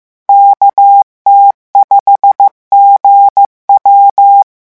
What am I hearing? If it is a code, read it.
Morse code, heard as a keyed tone, about 15 wpm, 800 Hz. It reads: KT5GW
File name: morse mp3 / revision13-vitesse 14 mots